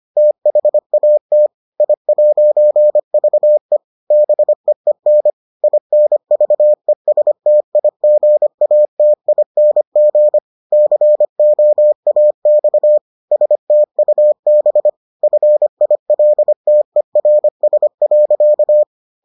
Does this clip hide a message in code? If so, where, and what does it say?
Yes: Morse code from 0.2 to 18.8 s, heard as a keyed tone, about 25 wpm, 600 Hz. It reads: THAT I'VE BEEN INVESTIGATING COAX STUB FILTERS.